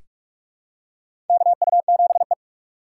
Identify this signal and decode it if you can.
Morse code, heard as a keyed tone, about 44 wpm, 700 Hz. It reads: KU7E
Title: USA Call Signs - 4 Characters - 3x Character Spacing - 45wpm